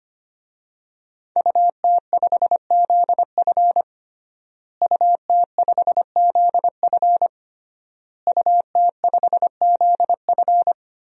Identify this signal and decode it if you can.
Morse code, heard as a keyed tone, about 25 wpm, 700 Hz. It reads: UT5ZF UT5ZF UT5ZF
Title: DXCC-100 Call Signs - 5 Characters - Repeated 3x - 25wpm